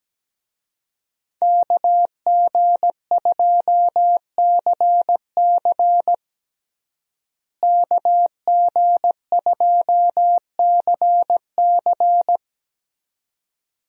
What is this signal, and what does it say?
Morse code, heard as a keyed tone, about 17 wpm, 700 Hz. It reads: KG2CC KG2CC
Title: USA Call Signs - 5 Characters - Repeated 2x - 17wpm